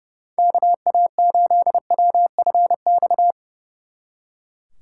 Morse code, heard as a keyed tone, about 37 wpm, 700 Hz. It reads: KA8WFX
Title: Speed Racing - USA 2x3 Call Signs 30wpm